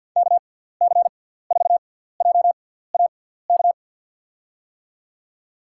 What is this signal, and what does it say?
Morse code, heard as a keyed tone, about 53 wpm, spaced as for 17 wpm, 700 Hz. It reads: KC4JAK
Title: USA Call Signs - 6 Characters - 6x Character Spacing - 50wpm